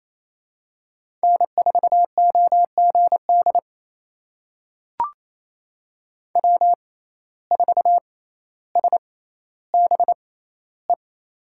Morse code, heard as a keyed tone, about 28 wpm, spaced as for 14 wpm, 700 Hz. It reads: N4OGD W4SBE
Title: USA Call Signs - 6x Character Spacing - 28wpm